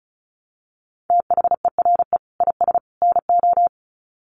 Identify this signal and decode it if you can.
Morse code, heard as a keyed tone, about 34 wpm, 700 Hz. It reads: THERE IS NO
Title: Sets of 3 Words 35wpm